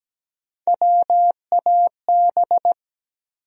Morse code, heard as a keyed tone, about 17 wpm, 700 Hz. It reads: WAB
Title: USA Call Sign Postfixes - 3 Letters 17wpm